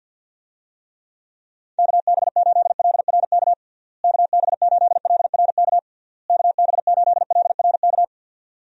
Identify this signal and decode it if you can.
Morse code, heard as a keyed tone, about 46 wpm, 700 Hz. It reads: KB8LRK KB8LRK KB8LRK
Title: USA Call Signs - 6 Characters - Repeated 3x - 50wpm